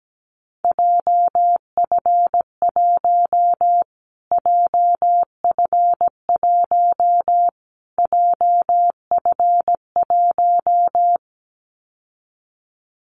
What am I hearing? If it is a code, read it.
Morse code, heard as a keyed tone, about 17 wpm, 700 Hz. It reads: JF1 JF1 JF1
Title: Callsign Prefixes for Top 25 DXCC - Repeated 3x - 17wpm